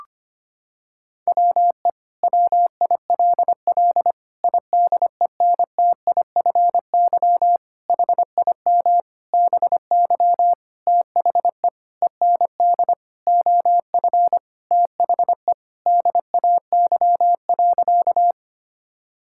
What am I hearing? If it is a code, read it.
Morse code, heard as a keyed tone, about 25 wpm, 700 Hz. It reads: WE WILL IDENTIFY HIM BY THE END OF THE DAY.
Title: Sentences from Top 1000 Words - Review Part 6 - 25wpm